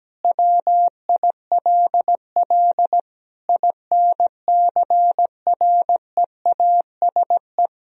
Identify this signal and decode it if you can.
Morse code, heard as a keyed tone, about 17 wpm, 700 Hz. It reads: WILL INCREASE
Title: Sentences from Top 1000 Words - Review Part 4 - 17wpm